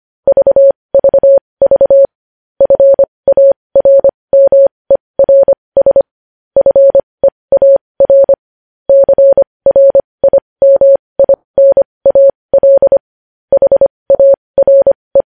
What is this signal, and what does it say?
Morse code, heard as a keyed tone, about 25 wpm, spaced as for 20 wpm, 550 Hz. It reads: VVV FARMERS FEAR CRIMINAL HARE
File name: podcast / News headlines-1x-25x20